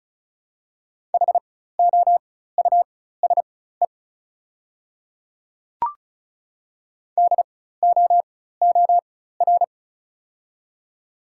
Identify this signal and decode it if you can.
Morse code, heard as a keyed tone, about 36 wpm, spaced as for 16 wpm, 690 Hz. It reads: HOUSE DOOR
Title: 4x Character Spacing - Top 500 Words - 35wpm